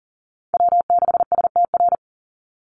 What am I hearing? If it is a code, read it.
Morse code, heard as a keyed tone, about 40 wpm, 700 Hz. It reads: W6STR
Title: NA SOTA Chaser Callsigns 40wpm